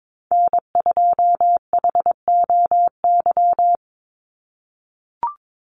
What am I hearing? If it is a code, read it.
Morse code, heard as a keyed tone, about 22 wpm, 700 Hz. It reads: N2HOY
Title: USA Call Signs 22wpm